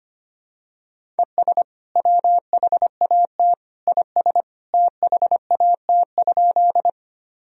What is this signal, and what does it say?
Morse code, heard as a keyed tone, about 25 wpm, 700 Hz. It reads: ES WHAT IS THAT?
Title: Sentences from Top 100 Words and Abbreviations 25wpm